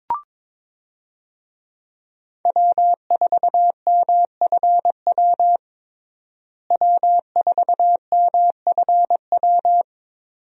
Morse code, heard as a keyed tone, about 22 wpm, 700 Hz. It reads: W4MFW W4MFW
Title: USA Call Signs - 5 Characters - Repeated 2x - 22wpm